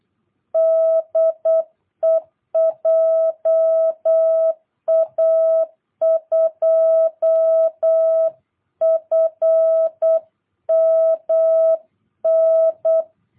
Morse code, heard as a keyed tone, about 8 wpm, 650 Hz. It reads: DE JA2FMN
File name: jh8cbh / 2022cw